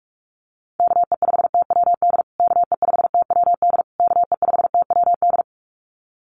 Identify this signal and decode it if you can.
Morse code, heard as a keyed tone, about 44 wpm, 700 Hz. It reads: KE5TWD KE5TWD KE5TWD
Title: USA Call Signs Repeated 3x 45wpm